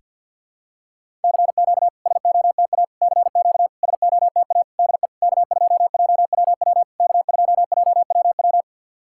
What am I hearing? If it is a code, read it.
Morse code, heard as a keyed tone, about 49 wpm, 690 Hz. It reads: CQ SOTA CQ SOTA DE K2JWW K2JWW
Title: SOTA QSO Practice 50wpm